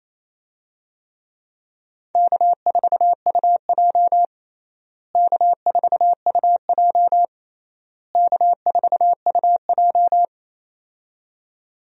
Morse code, heard as a keyed tone, about 28 wpm, 700 Hz. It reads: K4UJ K4UJ K4UJ
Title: USA Call Signs - 4 Characters - Repeated 3x - 28wpm